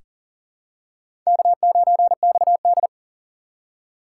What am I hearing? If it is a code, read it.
Morse code, heard as a keyed tone, about 40 wpm, 700 Hz. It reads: K9XD
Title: K1USN SST QSO Practice - 40wpm